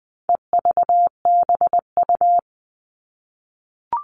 Morse code, heard as a keyed tone, about 20 wpm, 700 Hz. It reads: EVBU
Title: All Cut Numbers - 4-Digit 20wpm